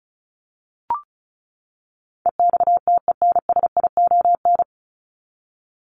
Morse code, heard as a keyed tone, about 34 wpm, 700 Hz. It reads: EXTENSION
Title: Top 1501-1750 Words 35wpm